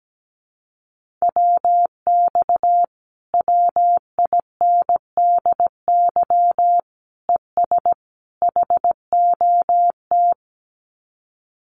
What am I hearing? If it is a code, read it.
Morse code, heard as a keyed tone, about 17 wpm, 700 Hz. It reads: WX WINDY ES HOT
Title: Common QSO Phrases Encore 17wpm